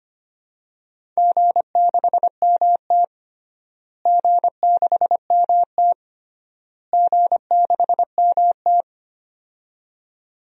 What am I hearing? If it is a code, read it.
Morse code, heard as a keyed tone, about 25 wpm, 700 Hz. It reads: G6MT G6MT G6MT
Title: DXCC-100 Call Signs - 4 Characters - Repeated 3x - 25wpm